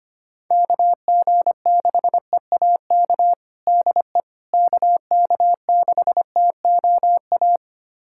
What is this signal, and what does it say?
Morse code, heard as a keyed tone, about 25 wpm, 700 Hz. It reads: KG6EAK DE KK6TOA
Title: USA Callsign DE Callsign - 6 Characters - 25wpm